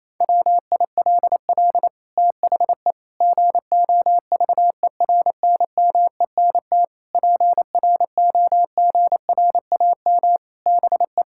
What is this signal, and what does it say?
Morse code, heard as a keyed tone, about 28 wpm, 700 Hz. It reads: WILL THE GOVERNMENT PROGRAM BE
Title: Sentences from Top 1000 Words - Review Part 5 - 28wpm